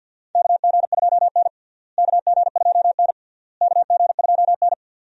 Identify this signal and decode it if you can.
Morse code, heard as a keyed tone, about 53 wpm, 700 Hz. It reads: KG2N KG2N KG2N
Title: USA Call Signs - 4 Characters - Repeated 3x - 50wpm